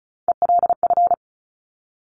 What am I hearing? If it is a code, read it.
Morse code, heard as a keyed tone, about 36 wpm, 700 Hz. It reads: ELF
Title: Top Three Letter Combinations in Words 35wpm